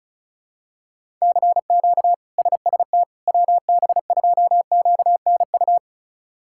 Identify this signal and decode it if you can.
Morse code, heard as a keyed tone, about 36 wpm, 700 Hz. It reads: CQ SST WB2QNU
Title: K1USN SST QSO Practice - 35wpm